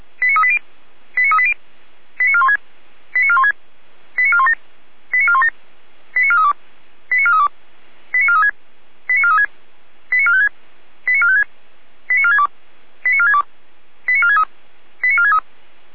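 Five-tone selective call (ZVEI), heard as quick runs of five tones, tones of about 70 ms. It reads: 89380 89380 89416 89416 89417 89417 89432 89432 89436 89436 89456 89456 89462 89462 89463 89463
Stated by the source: ZVEI five-tone call received by SDR.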